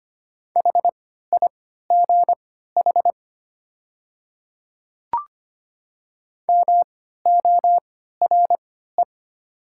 Morse code, heard as a keyed tone, about 25 wpm, spaced as for 14 wpm, 700 Hz. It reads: HIGH MORE
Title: 3x Character Spacing - Top 300 Words - 25wpm